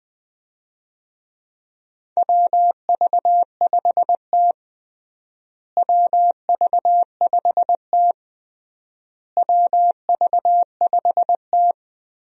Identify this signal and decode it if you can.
Morse code, heard as a keyed tone, about 20 wpm, 690 Hz. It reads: WV5T WV5T WV5T
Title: USA Call Signs - 4 Characters - Repeated 3x - 20wpm